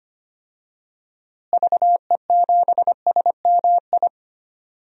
Morse code, heard as a keyed tone, about 25 wpm, 700 Hz. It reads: VE7SMI
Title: DXCC-100 Call Signs - 6 Characters - Repeated 2x - 25wpm